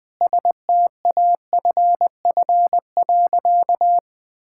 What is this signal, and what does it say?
Morse code, heard as a keyed tone, about 20 wpm, 700 Hz. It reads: STAFF.
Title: Sentences with 5-Word Phrases - 20wpm